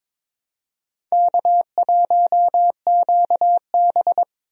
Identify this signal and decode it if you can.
Morse code, heard as a keyed tone, about 22 wpm, 700 Hz. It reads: K1QB
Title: USA Call Signs - 4 Characters - Repeated 2x - 22wpm